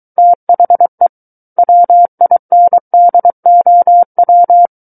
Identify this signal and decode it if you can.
Morse code, heard as a keyed tone, about 23 wpm, 700 Hz. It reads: THE WINDOW
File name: Book One / 14-Chapter 14 - 23 WPM